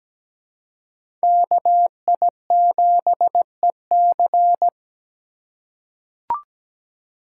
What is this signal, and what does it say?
Morse code, heard as a keyed tone, about 17 wpm, 700 Hz. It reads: KI7EC
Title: USA Call Signs - 3x Character Spacing - 17wpm